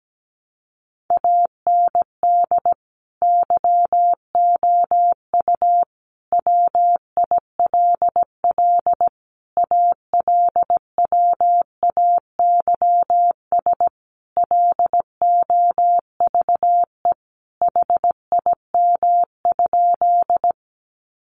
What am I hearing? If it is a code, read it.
Morse code, heard as a keyed tone, about 17 wpm, 700 Hz. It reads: AND YOU WILL ALWAYS LOVE HIM?
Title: Sentences from Top 400 Words 17wpm